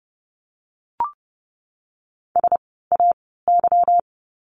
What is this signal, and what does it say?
Morse code, heard as a keyed tone, about 30 wpm, spaced as for 17 wpm, 700 Hz. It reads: SAY
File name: course / 30:10wpm / 237 Lesson 38f - Letter q - Words 30wpm